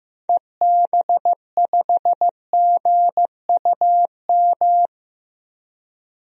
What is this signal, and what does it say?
Morse code, heard as a keyed tone, about 15 wpm, 700 Hz. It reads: EB5GUM
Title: DXCC-100 Call Signs - 6 Characters - 15wpm